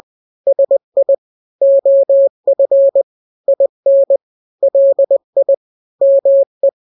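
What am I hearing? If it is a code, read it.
Morse code, heard as a keyed tone, about 20 wpm, 550 Hz. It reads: SI OF IN LI ME
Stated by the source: Morse drill of two-letter n-grams at 20/18 WPM (Farnsworth), from KK7RR.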